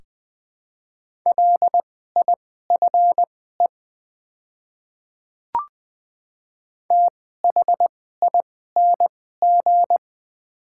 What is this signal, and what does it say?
Morse code, heard as a keyed tone, about 21 wpm, spaced as for 15 wpm, 700 Hz. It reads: LIFE THING
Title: 2x Character Spacing - Top 500 Words - 20wpm